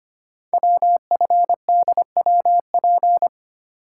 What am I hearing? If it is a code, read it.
Morse code, heard as a keyed tone, about 25 wpm, 700 Hz. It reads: WFDWP
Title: Code Groups - 5 Letters - 25wpm